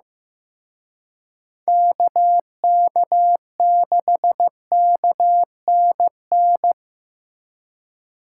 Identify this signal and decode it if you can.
Morse code, heard as a keyed tone, about 18 wpm, 700 Hz. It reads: KK6KNN
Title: Speed Racing - USA 2x3 Call Signs 15wpm